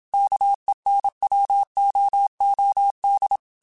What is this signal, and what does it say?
Morse code, heard as a keyed tone, about 27 wpm, 800 Hz. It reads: KENWOOD